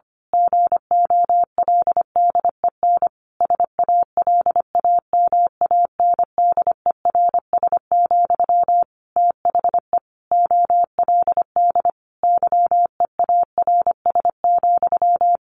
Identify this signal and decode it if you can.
Morse code, heard as a keyed tone, about 25 wpm, 700 Hz. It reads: GOLDEN SALAMANDERS, THE OLD YEARS,
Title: A Sound Of Thunder 25wpm